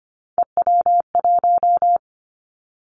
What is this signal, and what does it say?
Morse code, heard as a keyed tone, about 25 wpm, 700 Hz. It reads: EW1
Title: Callsign Prefixes for Top 100 DXCC - 25wpm